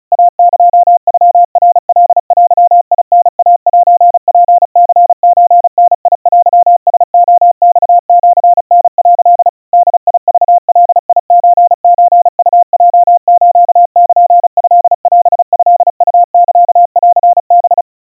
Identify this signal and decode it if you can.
Morse code (Japanese Wabun code), heard as a keyed tone, about 36 wpm, 700 Hz. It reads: イエノナカデタイセツニソダテラレマシタ」 ボクガソソウヲスルトオトウサンハ
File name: kawa / Morse sample / kamyu 130 cpm 700 Hz Crispy